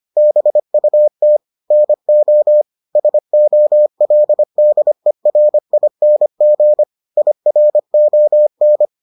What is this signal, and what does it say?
Morse code, heard as a keyed tone, about 25 wpm, 600 Hz. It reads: BUT NO SOLDERING IRON